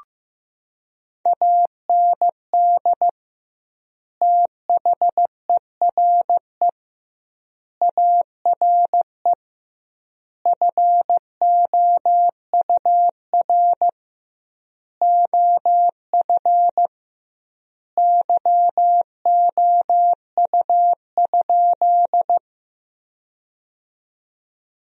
Morse code, heard as a keyed tone, about 15 wpm, 700 Hz. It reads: AND THERE ARE FOUR OF YOU?
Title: 2x Word-Spacing - Sentences from Top 300 Words - 15wpm